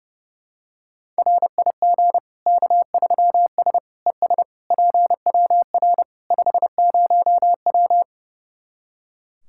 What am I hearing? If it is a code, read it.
Morse code, heard as a keyed tone, about 32 wpm, 700 Hz. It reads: RIG K3S ES PWR 50W
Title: Practice QSO - Exchange 2 - Info - 30wpm